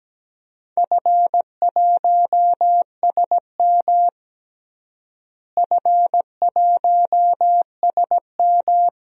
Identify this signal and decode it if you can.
Morse code, heard as a keyed tone, about 17 wpm, 700 Hz. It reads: F1SM F1SM
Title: DXCC-100 Call Signs - 4 Characters - Repeated 2x - 17wpm